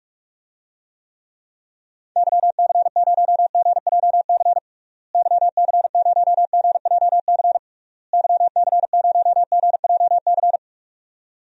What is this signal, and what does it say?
Morse code, heard as a keyed tone, about 46 wpm, 700 Hz. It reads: YC0GJC YC0GJC YC0GJC
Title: DXCC-100 Call Signs - 6 Characters - Repeated 3x - 45wpm